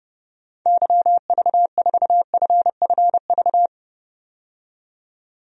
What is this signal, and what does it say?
Morse code, heard as a keyed tone, about 30 wpm, 700 Hz. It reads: YV4FFV
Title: DXCC-100 Call Signs - 6 Characters - 30wpm